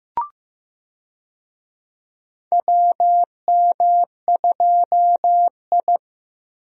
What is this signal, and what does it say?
Morse code, heard as a keyed tone, about 15 wpm, 700 Hz. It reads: WM2I
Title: USA Call Signs - 4 Characters - Repeated 2x - 15wpm